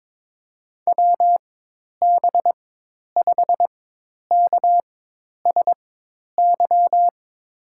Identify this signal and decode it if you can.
Morse code, heard as a keyed tone, about 22 wpm, spaced as for 10 wpm, 700 Hz. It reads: WB5KSY
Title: USA Call Signs - 6 Characters - 4x Character Spacing - 22wpm